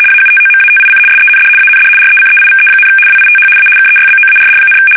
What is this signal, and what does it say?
RTTY, about 75 baud. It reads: AUTOMATIC SEND AND RECEIVE, WITH A KEYBOARD,
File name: fsk850-75B / teletype-corp